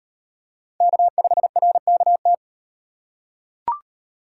Morse code, heard as a keyed tone, about 38 wpm, 700 Hz. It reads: K5RKT